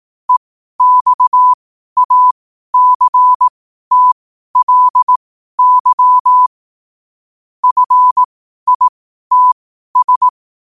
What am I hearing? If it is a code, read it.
Morse code, heard as a keyed tone, about 18 wpm, spaced as for 12 wpm, 1000 Hz. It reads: EXACTLY FITS
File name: qotdfiles / 12wpm